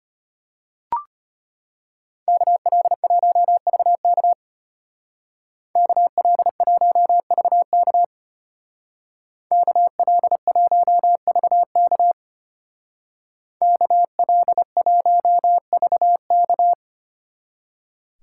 Morse code, heard as a keyed tone, about 30 wpm, 700 Hz. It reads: KL1VK KL1VK KL1VK KL1VK
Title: Speed Racing - USA 2x2 Call Signs 25wpm